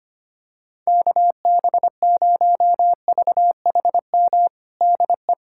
Morse code, heard as a keyed tone, about 25 wpm, 700 Hz. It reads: KB0VHM DE
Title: USA Callsign DE Callsign - 6 Characters - 25wpm